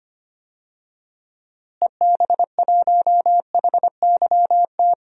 Morse code, heard as a keyed tone, about 25 wpm, 700 Hz. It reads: EB1HYT
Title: DXCC-100 Call Signs - 6 Characters - Repeated 3x - 25wpm